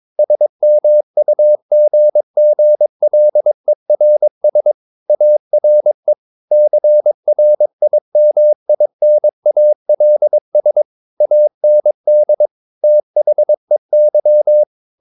Morse code, heard as a keyed tone, about 22 wpm, 600 Hz. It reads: SMUGGLERS ARE CRIMINALS AND THEY
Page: Romanen Never af Ken Follet på engelsk.